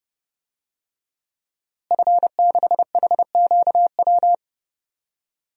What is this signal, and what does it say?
Morse code, heard as a keyed tone, about 30 wpm, 700 Hz. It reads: F6HQW
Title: DXCC-100 Call Signs - 5 Characters - Repeated 2x - 30wpm